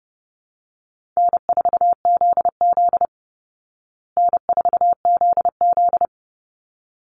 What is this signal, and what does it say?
Morse code, heard as a keyed tone, about 30 wpm, 700 Hz. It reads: N4ZZ N4ZZ
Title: Call Signs Repeated 3x - 30wpm